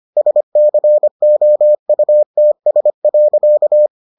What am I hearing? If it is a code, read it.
Morse code, heard as a keyed tone, about 25 wpm, 600 Hz. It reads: SCOUTS.